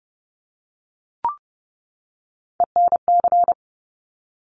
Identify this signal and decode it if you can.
Morse code, heard as a keyed tone, about 30 wpm, 700 Hz. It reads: ENC
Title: Top Three Letter Combinations in Words 30wpm